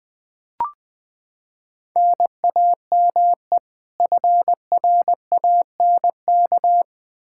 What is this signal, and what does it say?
Morse code, heard as a keyed tone, about 20 wpm, 700 Hz. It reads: NAME FRANK
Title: Practice QSO - Exchange 1 - Beginning - 20wpm